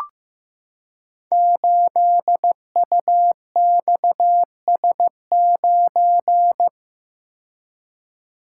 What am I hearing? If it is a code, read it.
Morse code, heard as a keyed tone, about 15 wpm, 700 Hz. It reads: 8UXS9
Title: Code Groups - 5 Letter:Numbers - 15wpm